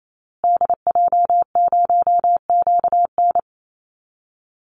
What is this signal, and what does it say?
Morse code, heard as a keyed tone, about 28 wpm, 700 Hz. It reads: DJ0QN
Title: Call Signs Repeated 2x - 28wpm